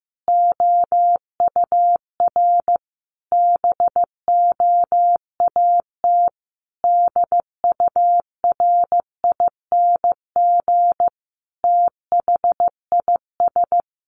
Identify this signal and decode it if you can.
Morse code, heard as a keyed tone, about 15 wpm, 700 Hz. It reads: OUR BOAT DURING THIS
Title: Sentences from Top 1500 Words 15wpm